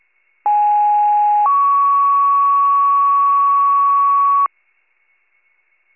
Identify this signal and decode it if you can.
Two-tone page, heard as a one-second tone, then a three-second tone, about 830 then 1120 Hz. It is heard from 0.5 to 4.5 s.